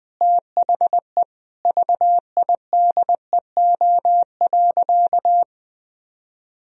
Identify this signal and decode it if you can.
Morse code, heard as a keyed tone, about 21 wpm, 700 Hz. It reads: THE VIDEO.
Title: Sentences from Top 1000 Words - Review Part 4 - 20wpm